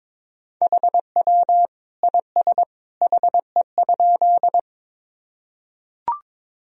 Morse code, heard as a keyed tone, about 22 wpm, 700 Hz. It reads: HW IS HE?
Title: Sentences from Top 100 Words and Abbreviations 22wpm